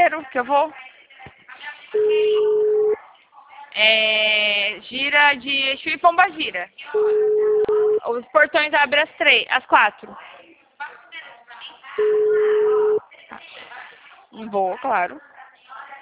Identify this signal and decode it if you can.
Ringback tone (Europe, 425 Hz), with speech.
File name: out- -2024- - - .29901